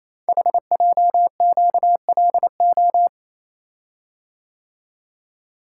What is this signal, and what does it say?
Morse code, heard as a keyed tone, about 28 wpm, 700 Hz. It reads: HJQLO
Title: Code Groups - 5 Letters - 28wpm